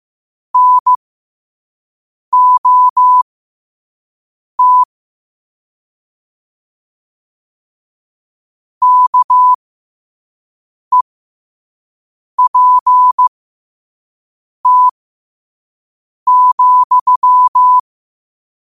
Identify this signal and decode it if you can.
Morse code, heard as a keyed tone, about 15 wpm, spaced as for 5 wpm, 1000 Hz. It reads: NOT KEPT,
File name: qotdfiles / 5wpm